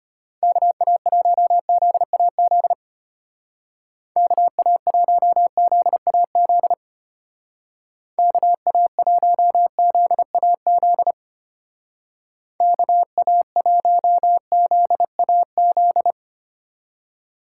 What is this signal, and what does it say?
Morse code, heard as a keyed tone, about 30 wpm, 700 Hz. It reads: KA1ZAZ KA1ZAZ KA1ZAZ KA1ZAZ
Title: Speed Racing - USA Call Signs 25wpm